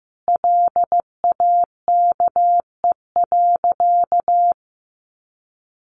Morse code, heard as a keyed tone, about 15 wpm, 700 Hz. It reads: LAKE.
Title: Sentences from Top 1100 Words 15wpm